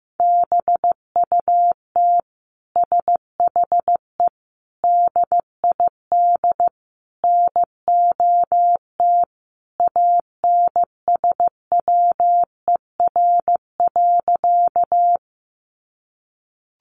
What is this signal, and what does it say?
Morse code, heard as a keyed tone, about 15 wpm, 700 Hz. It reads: BUT SHE DID NOT ANSWER.
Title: Sentences from Top 600 Words 15wpm